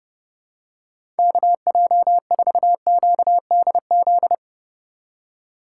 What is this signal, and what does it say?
Morse code, heard as a keyed tone, about 30 wpm, 700 Hz. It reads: KJ4QDZ
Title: USA Call Signs - 6 Characters - 4x Character Spacing - 30wpm